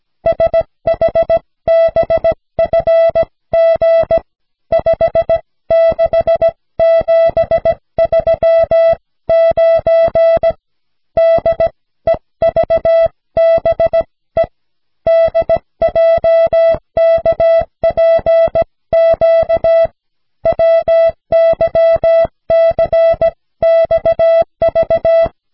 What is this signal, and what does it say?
Morse code, heard as a keyed tone, about 17 wpm, 660 Hz. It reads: SHBFG 56739 DEVBE DJKPQ WYCXV
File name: cw / snelheid 4